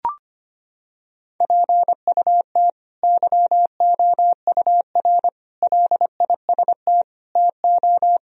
Morse code, heard as a keyed tone, about 25 wpm, 700 Hz. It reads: PUT YOUR LIST TO
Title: Sentences from Top 1000 Words - Review Part 2 - 25wpm